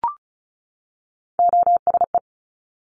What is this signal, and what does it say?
Morse code, heard as a keyed tone, about 36 wpm, 700 Hz. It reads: OSE
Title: Top Three Letter Combinations in Words 35wpm